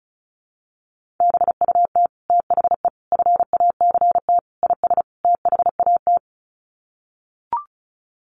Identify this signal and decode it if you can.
Morse code, heard as a keyed tone, about 34 wpm, 700 Hz. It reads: BUT THE FACT IS THAT
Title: Sets of 5 Words Encore2 35wpm